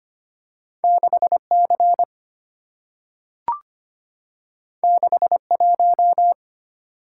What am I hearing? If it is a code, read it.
Morse code, heard as a keyed tone, about 25 wpm, 690 Hz. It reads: 6C 61
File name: Two Letter-Number Combinations - 25wpm